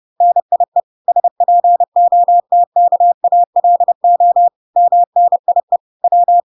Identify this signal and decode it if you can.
Morse code, heard as a keyed tone, about 30 wpm, 700 Hz. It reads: NIE SPOTKALO MNIE W